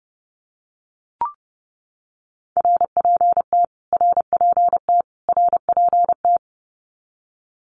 Morse code, heard as a keyed tone, about 30 wpm, 700 Hz. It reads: RPT RPT RPT
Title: Abbreviations Repeated 3x - 30wpm